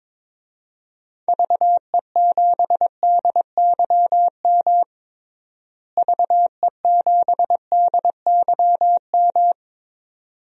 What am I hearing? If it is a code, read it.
Morse code, heard as a keyed tone, about 22 wpm, 700 Hz. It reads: VE7DYM VE7DYM
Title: DXCC-100 Call Signs - 6 Characters - Repeated 2x - 22wpm